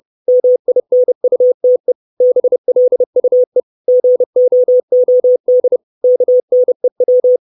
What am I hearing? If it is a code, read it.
Morse code, heard as a keyed tone, about 30 wpm, 500 Hz. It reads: MINUTE BLUE GOOD KNEW
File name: 632 Words 30WPM